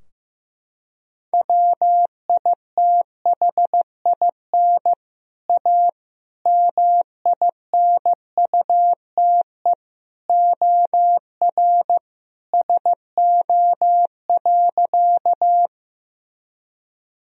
Morse code, heard as a keyed tone, about 16 wpm, 700 Hz. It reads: WITHIN A MINUTE OR SO.
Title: Sentences from Top 1000 Words - Review Part 4 - 15wpm